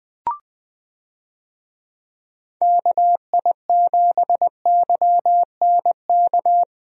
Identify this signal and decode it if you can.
Morse code, heard as a keyed tone, about 21 wpm, 700 Hz. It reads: KI7YNK
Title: USA Call Signs - 6 Characters - Repeated 3x - 20wpm